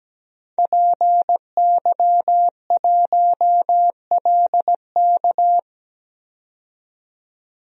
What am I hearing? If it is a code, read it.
Morse code, heard as a keyed tone, about 17 wpm, 700 Hz. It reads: PY1LK
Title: DXCC-100 Call Signs - 5 Characters - 17wpm